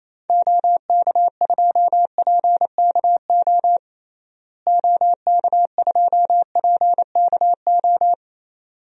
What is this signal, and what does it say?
Morse code, heard as a keyed tone, about 27 wpm, 700 Hz. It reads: OK2PKO OK2PKO
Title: DXCC-100 Call Signs - 6 Characters - Repeated 2x - 28wpm